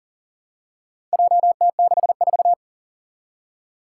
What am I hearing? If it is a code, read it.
Morse code, heard as a keyed tone, about 37 wpm, 700 Hz. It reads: JT64
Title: Four Challenging Letter-Number Combinations 40wpm